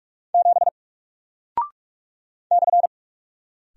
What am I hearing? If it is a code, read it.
Morse code, heard as a keyed tone, about 44 wpm, 700 Hz. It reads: 7 /
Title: Single Character (Letter:Number:Punctuation:Prosign) 45wpm